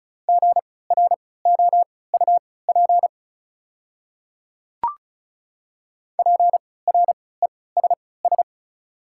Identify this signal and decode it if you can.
Morse code, heard as a keyed tone, about 36 wpm, spaced as for 20 wpm, 700 Hz. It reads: GROUP PRESS
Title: 3x Character Spacing - Top 300 Words - 35wpm